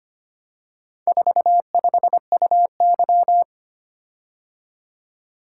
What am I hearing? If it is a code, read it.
Morse code, heard as a keyed tone, about 25 wpm, 700 Hz. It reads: 45UY
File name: practice / 25wpm / Four Letter-Number Combinations - 25wpm